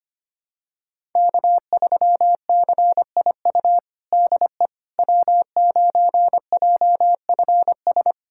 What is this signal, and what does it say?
Morse code, heard as a keyed tone, about 25 wpm, 700 Hz. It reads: K3CIU DE W9JFS
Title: USA Callsign DE Callsign - 5 Characters - 25wpm